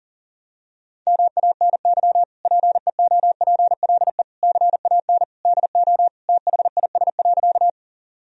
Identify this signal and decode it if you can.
Morse code, heard as a keyed tone, about 40 wpm, 700 Hz. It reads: MANY PEOPLE CAN DO THIS.
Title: Sentences from Top 1000 Words - Review Part 1 - 40wpm